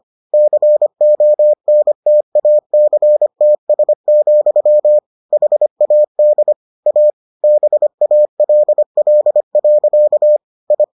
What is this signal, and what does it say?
Morse code, heard as a keyed tone, about 25 wpm, 600 Hz. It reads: CONTACTS, HAD A BALL. I